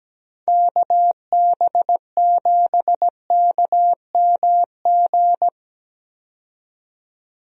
Morse code, heard as a keyed tone, about 17 wpm, 700 Hz. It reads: KB7KMG
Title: USA Call Signs - 6 Characters - 17wpm